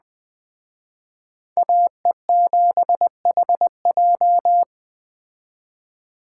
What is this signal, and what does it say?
Morse code, heard as a keyed tone, about 19 wpm, 700 Hz. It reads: AE7HJ